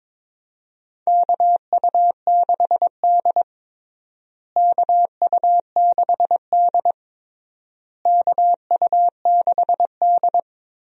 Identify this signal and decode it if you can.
Morse code, heard as a keyed tone, about 22 wpm, 700 Hz. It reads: KU6D KU6D KU6D
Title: USA Call Signs - 4 Characters - Repeated 3x - 22wpm